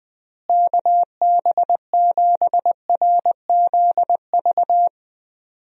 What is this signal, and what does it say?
Morse code, heard as a keyed tone, about 21 wpm, 700 Hz. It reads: KB7RZV
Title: USA Call Signs - 6 Characters - Repeated 2x - 20wpm